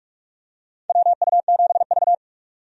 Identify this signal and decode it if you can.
Morse code, heard as a keyed tone, about 47 wpm, 700 Hz. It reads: WU7V
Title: USA Call Signs - 4 Characters - 45wpm